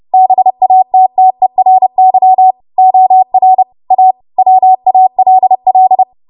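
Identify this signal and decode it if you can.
Morse code, heard as a keyed tone, about 30 wpm, 750 Hz. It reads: BATTERY OR A WALL
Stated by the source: W1AW Web Code Practice Files, 30 WPM run from ARRL; text from April 2023 QST.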